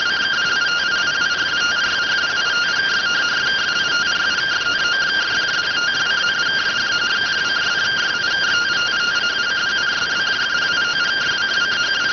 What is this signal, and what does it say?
Two-tone teleprinter signal, about 170 Hz shift.